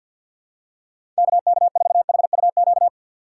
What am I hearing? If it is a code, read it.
Morse code, heard as a keyed tone, about 50 wpm, 700 Hz. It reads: KK4HUQ
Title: USA Call Signs - 6 Characters - 3x Character Spacing - 50wpm